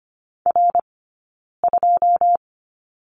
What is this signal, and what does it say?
Morse code, heard as a keyed tone, about 25 wpm, spaced as for 9 wpm, 690 Hz. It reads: R2
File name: course / 25:10wpm / 092 Lesson 16d - Letter c - Two Characters 25wpm